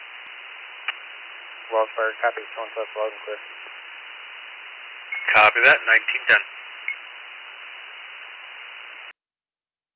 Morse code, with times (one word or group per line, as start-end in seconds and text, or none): none